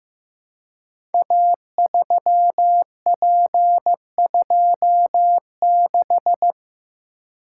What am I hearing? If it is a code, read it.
Morse code, heard as a keyed tone, about 15 wpm, 700 Hz. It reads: A3P26
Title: Code Groups - 5 Letter:Numbers - 15wpm